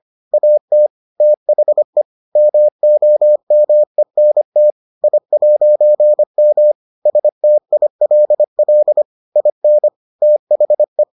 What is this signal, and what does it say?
Morse code, heard as a keyed tone, about 25 wpm, 600 Hz. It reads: AT THE MOMENT I'M STILL IN THE